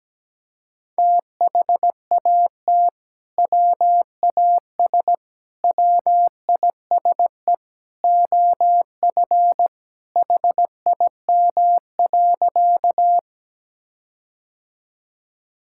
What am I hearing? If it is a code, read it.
Morse code, heard as a keyed tone, about 17 wpm, 700 Hz. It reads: THAT WAS WISE OF HIM.
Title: Sentences from Top 1600 Words 17wpm